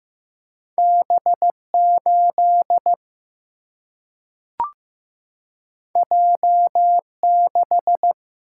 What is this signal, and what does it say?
Morse code, heard as a keyed tone, about 15 wpm, 700 Hz. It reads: B8 J6
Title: Single-Letter Single-Number 15wpm